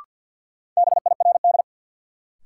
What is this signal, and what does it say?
Morse code, heard as a keyed tone, about 53 wpm, 700 Hz. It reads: BIRD
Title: Sentences from Top 400 Words - New Word Drill - 50wpm